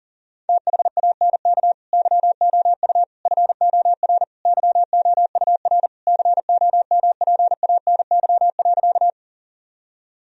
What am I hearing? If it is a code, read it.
Morse code, heard as a keyed tone, about 40 wpm, 700 Hz. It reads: THANK YOU FOR YOUR COMPANY.
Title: Sentences from Top 1000 Words - Review Part 3 - 40wpm